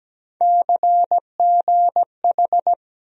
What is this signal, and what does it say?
Morse code, heard as a keyed tone, about 17 wpm, 700 Hz. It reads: CGH
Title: USA Call Sign Postfixes - 3 Letters 17wpm